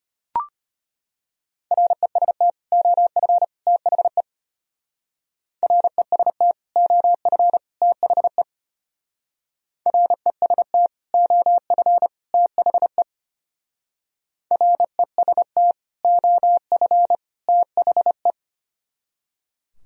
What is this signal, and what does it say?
Morse code, heard as a keyed tone, about 31 wpm, 700 Hz. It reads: REST OF THE REST OF THE REST OF THE REST OF THE